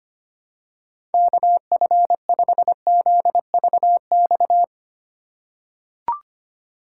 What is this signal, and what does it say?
Morse code, heard as a keyed tone, about 25 wpm, 700 Hz. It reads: KF5ZVX
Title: USA Call Signs - 6 Characters - 3x Character Spacing - 25wpm